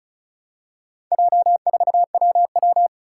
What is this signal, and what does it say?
Morse code, heard as a keyed tone, about 34 wpm, 700 Hz. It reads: J4WW
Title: Four Challenging Letter-Number Combinations 35wpm